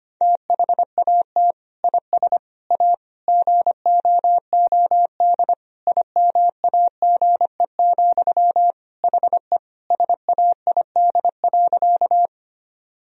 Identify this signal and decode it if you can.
Morse code, heard as a keyed tone, about 25 wpm, 700 Hz. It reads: THAT IS A GOOD IMAGE, HE SAID.
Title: Sentences from Top 1000 Words - Review Part 4 - 25wpm